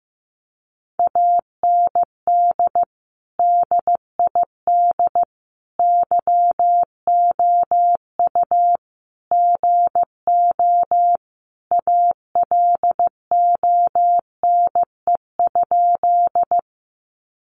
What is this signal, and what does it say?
Morse code, heard as a keyed tone, about 15 wpm, 700 Hz. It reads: AND DID YOU GO ALONE?
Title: Sentences from Top 1000 Words 15wpm